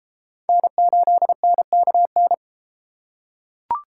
Morse code, heard as a keyed tone, about 30 wpm, 700 Hz. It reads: N8NKN